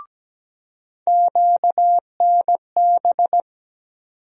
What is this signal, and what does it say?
Morse code, heard as a keyed tone, about 17 wpm, 700 Hz. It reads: QNB
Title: USA Call Sign Postfixes - 3 Letters 17wpm